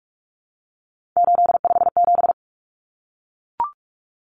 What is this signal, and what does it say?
Morse code, heard as a keyed tone, about 44 wpm, 700 Hz. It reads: 857
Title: Common Cut Numbers - 3-Digit 45wpm